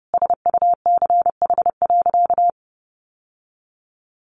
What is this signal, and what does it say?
Morse code, heard as a keyed tone, about 30 wpm, 700 Hz. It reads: SUCH.
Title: Sentences from Top 1300 Words 30wpm